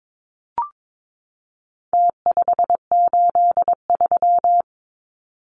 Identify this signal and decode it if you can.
Morse code, heard as a keyed tone, about 22 wpm, 700 Hz. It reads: T583
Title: Numbers - Contest Style - 4-Digit 22wpm